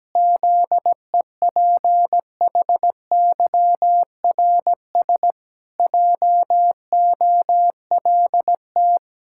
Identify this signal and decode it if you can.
Morse code, heard as a keyed tone, about 17 wpm, 700 Hz. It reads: ZEPHYRS JOLT
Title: Pangram Challenge v2 - 17wpm